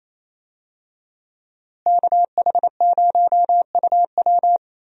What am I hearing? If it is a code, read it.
Morse code, heard as a keyed tone, about 28 wpm, 700 Hz. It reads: KH0UW
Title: USA Call Signs - 5 Characters - Repeated 3x - 28wpm